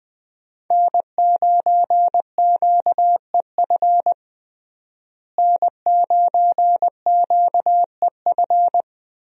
Morse code, heard as a keyed tone, about 20 wpm, 700 Hz. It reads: N9QEF N9QEF
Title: USA Call Signs - 5 Characters - Repeated 2x - 20wpm